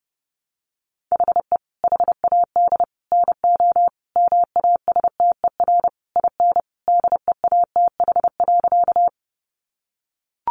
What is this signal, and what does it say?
Morse code, heard as a keyed tone, about 30 wpm, 700 Hz. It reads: HE HAD NO MASTER IN DEATH.
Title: Sentences from Top 1300 Words 30wpm